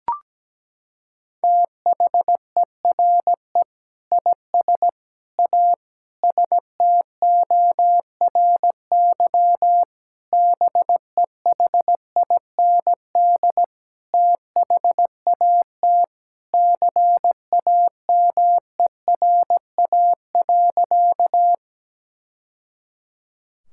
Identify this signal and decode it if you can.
Morse code, heard as a keyed tone, about 17 wpm, 700 Hz. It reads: THERE IS A STORY BEHIND THAT CAMERA.
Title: Sentences from Top 1000 Words - Review Part 5 - 17wpm